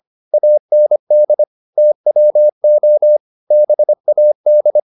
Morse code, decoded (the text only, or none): AND TWO BAD